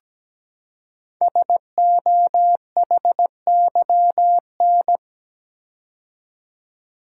Morse code, heard as a keyed tone, about 17 wpm, 700 Hz. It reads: SOHYN
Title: Code Groups - 5 Letters - 17wpm